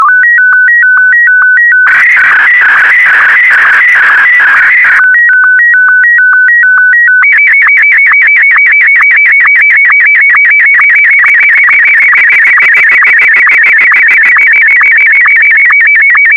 SSTV (slow-scan television), Martin 1.